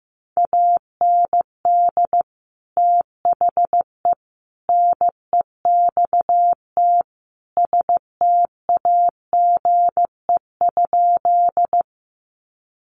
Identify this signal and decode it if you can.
Morse code, heard as a keyed tone, about 15 wpm, 700 Hz. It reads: AND THE NEXT STAGE?
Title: Sentences from Top 1200 Words 15wpm